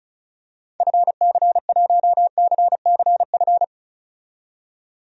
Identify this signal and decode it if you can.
Morse code, heard as a keyed tone, about 34 wpm, 700 Hz. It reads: FC1CCF
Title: DXCC-100 Call Signs - 6 Characters - 35wpm